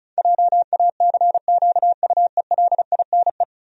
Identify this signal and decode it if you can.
Morse code, heard as a keyed tone, about 36 wpm, 700 Hz. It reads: JACQUELINE
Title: USA Common Names - 35wpm